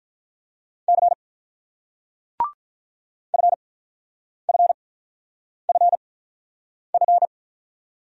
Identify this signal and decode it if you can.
Morse code, heard as a keyed tone, about 49 wpm, spaced as for 9 wpm, 690 Hz. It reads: C FFFF